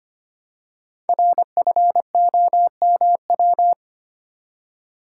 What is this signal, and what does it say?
Morse code, heard as a keyed tone, about 25 wpm, 700 Hz. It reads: RFOMW
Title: Code Groups - 5 Letters - 25wpm